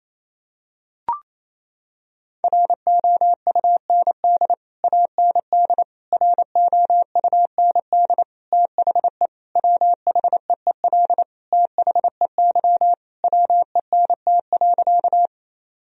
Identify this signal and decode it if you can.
Morse code, heard as a keyed tone, about 27 wpm, 700 Hz. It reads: ROUND AND ROUND THE WHEEL THEY WENT.
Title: Sentences from Top 1000 Words - Review Part 3 - 28wpm